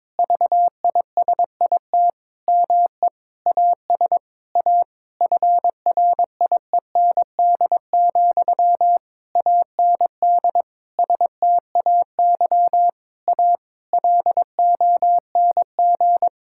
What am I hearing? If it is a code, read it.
Morse code, heard as a keyed tone, about 22 wpm, 700 Hz. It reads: VISIT ME AS A FRIEND, AND STAY A LONG
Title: Sentences from Top 1000 Words - Review Part 6 - 22wpm